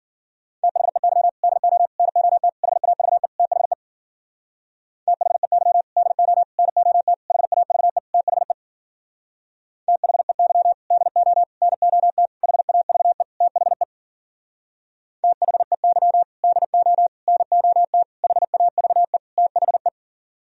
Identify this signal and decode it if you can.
Morse code, heard as a keyed tone, about 51 wpm, 700 Hz. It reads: THEY DO NOT HAVE THE THEY DO NOT HAVE THE THEY DO NOT HAVE THE THEY DO NOT HAVE THE